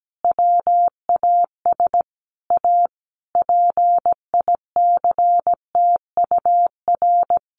Morse code, heard as a keyed tone, about 17 wpm, 700 Hz. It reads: WAS A PICTUR
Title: Sentences from Top 500 Words 17wpm